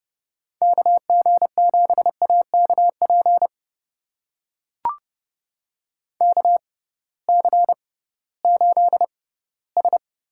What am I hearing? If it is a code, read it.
Morse code, heard as a keyed tone, about 31 wpm, spaced as for 18 wpm, 700 Hz. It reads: KG7AKP KC8S
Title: USA Call Signs - 6 Characters - 6x Character Spacing - 30wpm